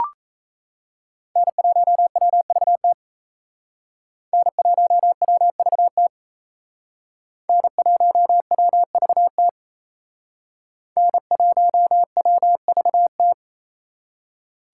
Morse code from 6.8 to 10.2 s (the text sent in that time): N1WVT